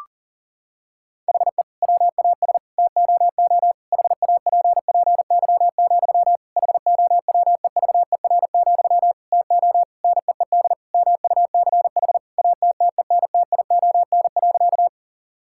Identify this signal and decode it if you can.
Morse code, heard as a keyed tone, about 40 wpm, 700 Hz. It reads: HE WAS TOO HAPPY, HOWEVER, TO NEED MUCH ATTENTION.
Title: Sentences from Top 1000 Words - Review Part 6 - 40wpm